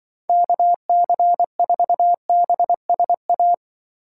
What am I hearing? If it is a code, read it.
Morse code, heard as a keyed tone, about 23 wpm, 700 Hz. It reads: KC4BSA
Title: Speed Racing - USA Call Signs 20wpm